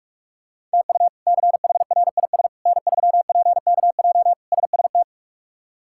Morse code, heard as a keyed tone, about 46 wpm, 700 Hz. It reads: TU CHRIS N3PKJ SST
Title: K1USN SST QSO Practice - 45wpm